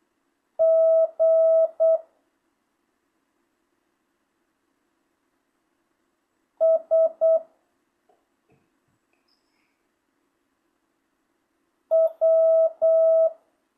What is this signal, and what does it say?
Morse code, heard as a keyed tone, about 8 wpm, 650 Hz. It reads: G S W
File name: jh8cbh / 2025cw